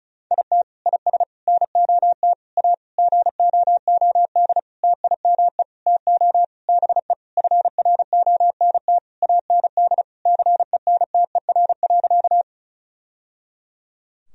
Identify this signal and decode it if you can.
Morse code, heard as a keyed tone, about 34 wpm, 700 Hz. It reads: IT IS NOT A GOOD TIME TO BE FRONT AND CENTER.
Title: Sentences from Top 1000 Words - Review Part 2 - 35wpm